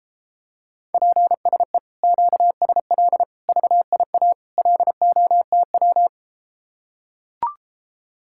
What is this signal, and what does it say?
Morse code, heard as a keyed tone, about 33 wpm, 700 Hz. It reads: PSE QSL VIA LOTW